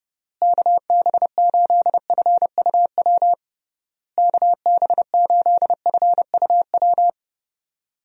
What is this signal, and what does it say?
Morse code, heard as a keyed tone, about 32 wpm, 700 Hz. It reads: KB8FUW KB8FUW
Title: USA Call Signs - 6 Characters - Repeated 3x - 30wpm